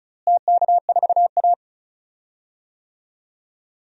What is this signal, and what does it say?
Morse code, heard as a keyed tone, about 36 wpm, 700 Hz. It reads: TK4A
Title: DXCC-100 Call Signs - 4 Characters - Repeated 2x - 35wpm